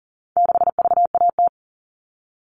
Morse code, heard as a keyed tone, about 40 wpm, 700 Hz. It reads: 6VAT
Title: All Cut Numbers - 4-Digit 40wpm